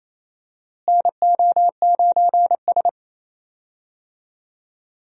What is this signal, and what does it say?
Morse code, heard as a keyed tone, about 28 wpm, 700 Hz. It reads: NO9S
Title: USA Call Signs - 4 Characters - Repeated 3x - 28wpm